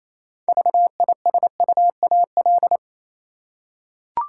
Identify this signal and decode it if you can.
Morse code, heard as a keyed tone, about 28 wpm, 700 Hz. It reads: VISUAL
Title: Six Letter Words - 28wpm